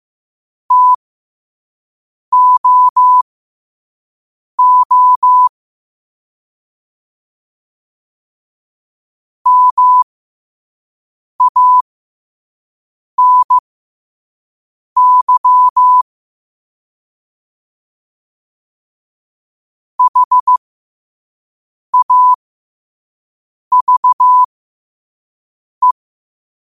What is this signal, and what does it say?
Morse code, heard as a keyed tone, about 15 wpm, spaced as for 5 wpm, 1000 Hz. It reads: TOO MANY HAVE
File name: qotdfiles / 5wpm